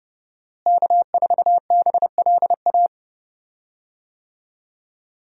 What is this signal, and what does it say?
Morse code, heard as a keyed tone, about 30 wpm, 700 Hz. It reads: K4BLA
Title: USA Call Signs - 5 Characters - Repeated 3x - 30wpm